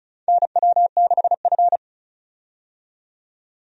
Morse code, heard as a keyed tone, about 34 wpm, 700 Hz. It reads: NW6F
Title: USA Call Signs - 4 Characters - 35wpm